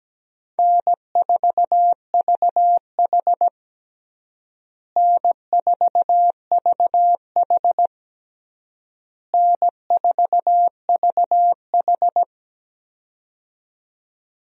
Morse code, heard as a keyed tone, about 17 wpm, 700 Hz. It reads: N4VH N4VH N4VH
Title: USA Call Signs - 4 Characters - Repeated 3x - 17wpm